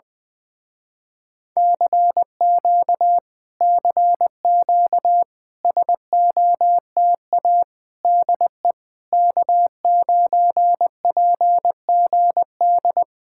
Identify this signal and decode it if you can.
Morse code, heard as a keyed tone, about 20 wpm, 700 Hz. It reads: CQ CQ SOTA DE K9PGD
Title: Speed-Racing Common QSO Phrases Encore - 17wpm